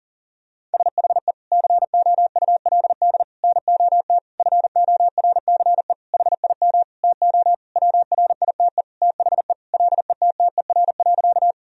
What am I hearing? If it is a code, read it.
Morse code, heard as a keyed tone, about 40 wpm, 700 Hz. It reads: SHE COULD NOT FORCE HIM TO WRITE THE LETTER.
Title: Sentences from Top 1000 Words - Review Part 3 - 40wpm